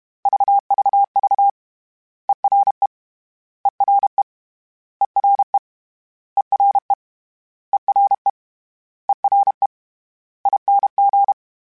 Morse code, heard as a keyed tone, about 32 wpm, 800 Hz. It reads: VVV ERE ERE ERE ERE ERE ERE ING
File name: Special INGs / Special INGs-mp3s / Ere ing @30